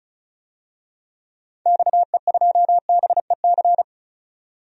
Morse code, heard as a keyed tone, about 34 wpm, 700 Hz. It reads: XE2BEC
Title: DXCC-100 Call Signs - 6 Characters - 35wpm